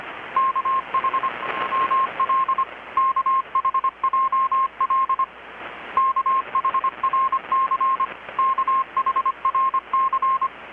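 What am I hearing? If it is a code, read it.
Morse code, heard as a keyed tone, about 25 wpm, 1060 Hz, with radio noise. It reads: KHML KHJL KHRC KHRC
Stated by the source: Morse from KPH's Night of Nights XX maritime radiotelegraphy event, received in Chiapas, Mexico about 3,800 km away on a Kenwood R-600 with a grounded 10 m long-wire antenna.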